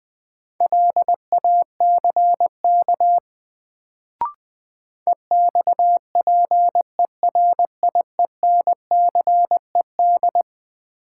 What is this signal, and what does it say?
Morse code, heard as a keyed tone, about 20 wpm, 700 Hz. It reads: LACK EXPERIENCED
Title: Sentences from Top 1100 Words - New Word Drill Alt - 20wpm